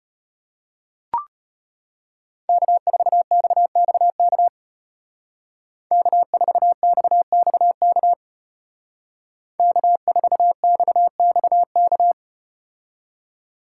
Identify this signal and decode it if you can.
Morse code, heard as a keyed tone, about 32 wpm, 700 Hz. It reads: K4XXK K4XXK K4XXK